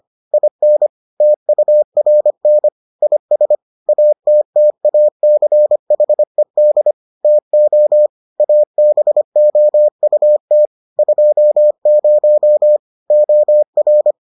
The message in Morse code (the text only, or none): IN TURN IS ATTACHED TO ABOUT 20 OR